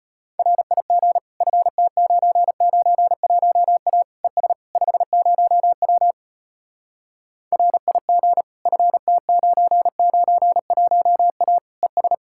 Morse code, heard as a keyed tone, about 32 wpm, 700 Hz. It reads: RIG FT991A ES 50W RIG FT991A ES